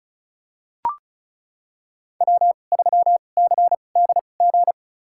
Morse code, heard as a keyed tone, about 36 wpm, spaced as for 25 wpm, 700 Hz. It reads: W3CDG
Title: USA Call Signs - 2x Character Spacing - 35wpm